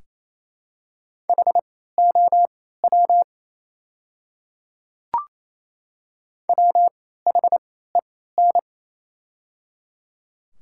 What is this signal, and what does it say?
Morse code, heard as a keyed tone, about 28 wpm, spaced as for 16 wpm, 700 Hz. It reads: HOW WHEN
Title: 3x Character Spacing - Top 100 Words - 28wpm